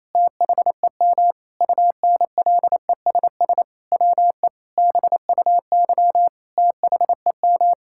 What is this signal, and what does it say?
Morse code, heard as a keyed tone, about 27 wpm, 700 Hz. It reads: THEM UNLESS WE BUY THEM
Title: Sentences from Top 1000 Words - Review Part 6 - 28wpm